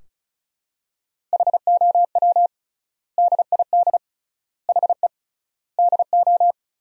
Morse code, heard as a keyed tone, about 36 wpm, 700 Hz. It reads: HOW DID HE DO
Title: 3x Word-Spacing - Sentences from Top 100 Words - 35wpm